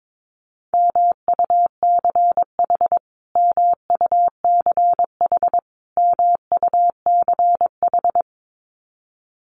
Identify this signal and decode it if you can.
Morse code, heard as a keyed tone, about 22 wpm, 700 Hz. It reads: MUCH MUCH MUCH
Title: Top 500 Words Repeated 22wpm